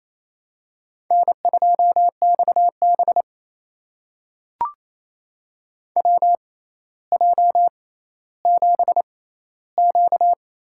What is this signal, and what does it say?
Morse code, heard as a keyed tone, about 29 wpm, spaced as for 14 wpm, 700 Hz. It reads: N2XB WJ7Q
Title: USA Call Signs - 4 Characters - 6x Character Spacing - 28wpm